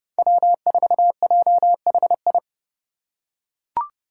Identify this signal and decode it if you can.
Morse code, heard as a keyed tone, about 30 wpm, 700 Hz. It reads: W4JHI
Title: Five Challenging Letter-Number Combinations 30wpm